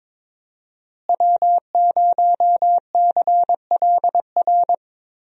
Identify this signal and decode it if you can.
Morse code, heard as a keyed tone, about 21 wpm, 700 Hz. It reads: W0CLR
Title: USA Call Signs - 5 Characters - Repeated 3x - 22wpm